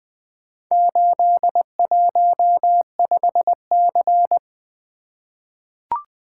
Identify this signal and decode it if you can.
Morse code, heard as a keyed tone, about 20 wpm, 700 Hz. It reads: 815C
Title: Four Letter-Number Combinations - 20wpm